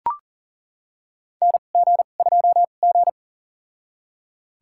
Morse code, heard as a keyed tone, about 37 wpm, spaced as for 28 wpm, 690 Hz. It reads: NG2G
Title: USA Call Signs - 4 Characters - 2x Character Spacing - 40wpm